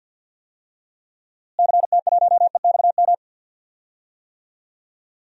Morse code, heard as a keyed tone, about 52 wpm, 700 Hz. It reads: CT1EXM
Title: DXCC-100 Call Signs - 6 Characters - Repeated 2x - 50wpm